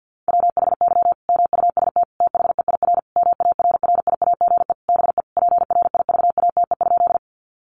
Morse code, heard as a keyed tone, about 50 wpm, 700 Hz. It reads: WHY MUST THEIR MARRIAGE BE PRIVATE?
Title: Sentences from Top 1000 Words 50wpm